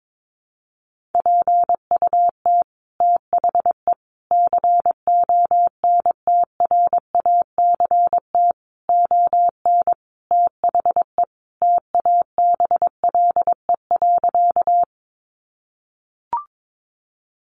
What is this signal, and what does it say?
Morse code, heard as a keyed tone, about 22 wpm, 700 Hz. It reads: PUT THE CONTRACT ON THE TABLE.
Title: Sentences from Top 1400 Words 22wpm